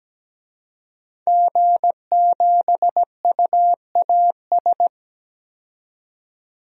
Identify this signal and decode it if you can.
Morse code, heard as a keyed tone, about 17 wpm, 700 Hz. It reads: G7UAS
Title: DXCC-100 Call Signs - 5 Characters - 17wpm